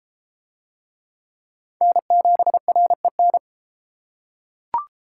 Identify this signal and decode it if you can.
Morse code, heard as a keyed tone, about 33 wpm, 700 Hz. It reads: N7REN